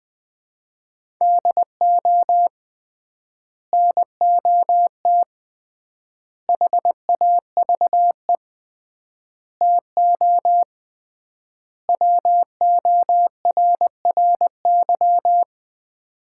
Morse code, heard as a keyed tone, about 21 wpm, 700 Hz. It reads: DO NOT HAVE TO WORRY
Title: 3x Word-Spacing - Sets of 5 Words - 20wpm